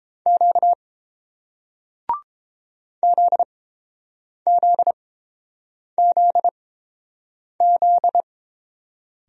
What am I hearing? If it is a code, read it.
Morse code, heard as a keyed tone, about 29 wpm, spaced as for 7 wpm, 690 Hz. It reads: Q ZZZZ